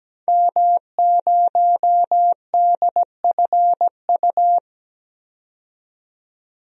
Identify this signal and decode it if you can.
Morse code, heard as a keyed tone, about 17 wpm, 700 Hz. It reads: M0DFU
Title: DXCC-100 Call Signs - 5 Characters - 17wpm